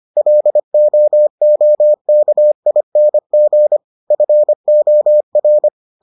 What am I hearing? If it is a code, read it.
Morse code, heard as a keyed tone, about 25 wpm, 600 Hz. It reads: LOOKING FOR